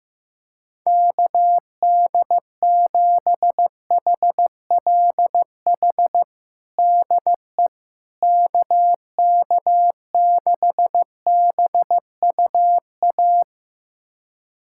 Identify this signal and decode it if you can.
Morse code, heard as a keyed tone, about 16 wpm, 700 Hz. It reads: KD7HLH DE KK6BUA
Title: USA Callsign DE Callsign - 6 Characters - 15wpm